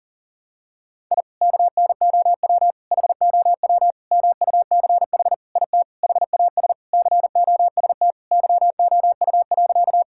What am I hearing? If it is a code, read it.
Morse code, heard as a keyed tone, about 40 wpm, 700 Hz. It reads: I KNOW HOW MUCH IT HAS COST YOU.
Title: Sentences from Top 1000 Words - Review Part 5 - 40wpm